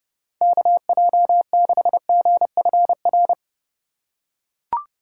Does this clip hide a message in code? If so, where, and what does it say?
Yes: Morse code from 0.4 to 3.3 s, heard as a keyed tone, about 30 wpm, 700 Hz. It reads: KJ6GFR